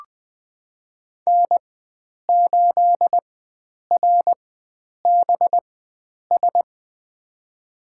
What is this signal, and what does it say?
Morse code, heard as a keyed tone, about 20 wpm, spaced as for 9 wpm, 700 Hz. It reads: N8RBS
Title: USA Call Signs - 4x Character Spacing - 20wpm